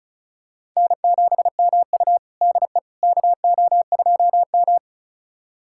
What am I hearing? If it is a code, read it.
Morse code, heard as a keyed tone, about 36 wpm, 700 Hz. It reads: N7MU DE KO2M
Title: USA Callsign DE Callsign - 4 Characters - 35wpm